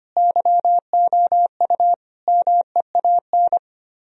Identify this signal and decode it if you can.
Morse code, heard as a keyed tone, about 31 wpm, 700 Hz. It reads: YOU MEAN
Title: Speed-Racing - Sets of 4 Words 25wpm